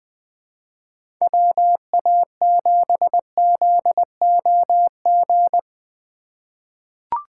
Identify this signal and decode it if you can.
Morse code, heard as a keyed tone, about 21 wpm, 700 Hz. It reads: WA7ZOG
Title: USA Call Signs - 6 Characters - 4x Character Spacing - 20wpm